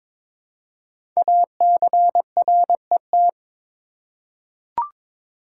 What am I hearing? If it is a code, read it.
Morse code, heard as a keyed tone, about 23 wpm, 700 Hz. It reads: ACRET
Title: Code Groups - 5 Letters - 22wpm